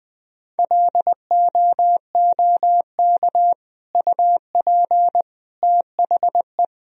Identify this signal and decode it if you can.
Morse code, heard as a keyed tone, about 20 wpm, 700 Hz. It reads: LOOK UP THE
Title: Sentences from Top 1000 Words - Review Part 4 - 20wpm